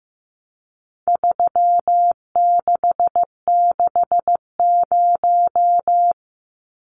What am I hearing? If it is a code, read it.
Morse code, heard as a keyed tone, about 15 wpm, 700 Hz. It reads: 3660
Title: Numbers - 4-Digit 15wpm